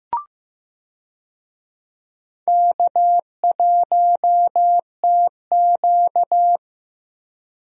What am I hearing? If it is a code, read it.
Morse code, heard as a keyed tone, about 15 wpm, 700 Hz. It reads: K1TQ
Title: USA Call Signs - 4 Characters - Repeated 2x - 15wpm